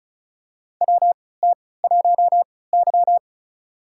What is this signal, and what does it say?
Morse code, heard as a keyed tone, about 36 wpm, spaced as for 20 wpm, 700 Hz. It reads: WT1Y
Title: USA Call Signs - 3x Character Spacing - 35wpm